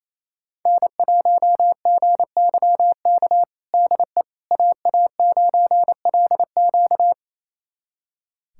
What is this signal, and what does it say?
Morse code, heard as a keyed tone, about 27 wpm, 700 Hz. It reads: N1GYK DE AA9LQ
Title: USA Callsign DE Callsign - 5 Characters - 28wpm